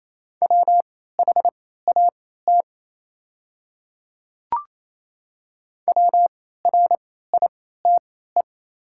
Morse code, heard as a keyed tone, about 27 wpm, spaced as for 16 wpm, 690 Hz. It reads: WHAT WRITE
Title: 3x Character Spacing - Top 100 Words - 28wpm